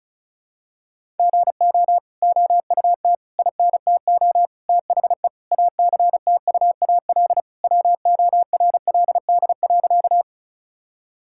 Morse code, heard as a keyed tone, about 36 wpm, 700 Hz. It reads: GO OUT INTO THE ACTUAL WORLD.
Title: Sentences from Top 1000 Words - Review Part 5 - 35wpm